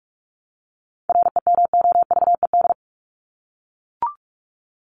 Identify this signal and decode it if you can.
Morse code, heard as a keyed tone, about 44 wpm, 700 Hz. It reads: REMOVED
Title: ED Suffix - Top 100 Words with Base Words 45wpm